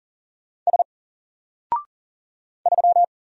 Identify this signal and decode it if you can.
Morse code, heard as a keyed tone, about 40 wpm, 700 Hz. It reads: S 3
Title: Single Letter-Number Alt 40wpm